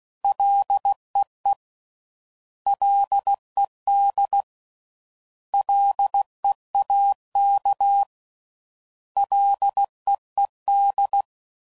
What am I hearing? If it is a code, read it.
Morse code, heard as a keyed tone, about 16 wpm, 800 Hz. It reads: LEE LED LEAK LEED